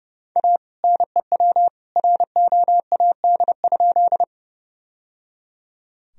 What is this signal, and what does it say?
Morse code, heard as a keyed tone, about 30 wpm, 700 Hz. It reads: A NEW ROAD?
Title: Sentences from Top 1000 Words - Review Part 2 - 30wpm